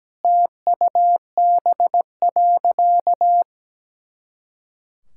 Morse code, heard as a keyed tone, about 17 wpm, 700 Hz. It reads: TUB.
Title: Pangram Challenge v2 - 17wpm